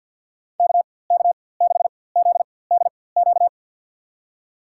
Morse code, heard as a keyed tone, about 53 wpm, spaced as for 24 wpm, 700 Hz. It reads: KK6ZDQ
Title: USA Call Signs - 6 Characters - 4x Character Spacing - 50wpm